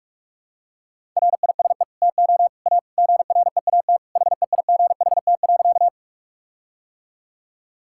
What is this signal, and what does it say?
Morse code, heard as a keyed tone, about 46 wpm, 700 Hz. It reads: RISE TO A GREAT HEIGHT.
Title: Sentences from Top 1600 Words 45wpm